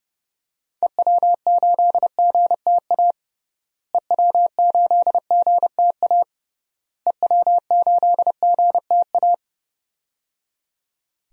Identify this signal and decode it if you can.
Morse code, heard as a keyed tone, about 31 wpm, 700 Hz. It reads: EW8GTA EW8GTA EW8GTA
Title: DXCC-100 Call Signs - 6 Characters - Repeated 3x - 30wpm